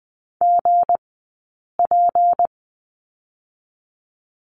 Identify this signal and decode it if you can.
Morse code, heard as a keyed tone, about 20 wpm, 700 Hz. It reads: G P
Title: 158 Lesson 26d - Letter g - Two Characters 20wpm